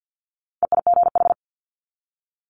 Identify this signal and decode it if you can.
Morse code, heard as a keyed tone, about 50 wpm, 700 Hz. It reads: EIGH
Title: Multi-letter Phonemes 50wpm